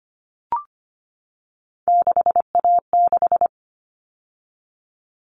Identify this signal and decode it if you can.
Morse code, heard as a keyed tone, about 25 wpm, 700 Hz. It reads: 6A6
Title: All Cut Numbers - 3-Digit 25wpm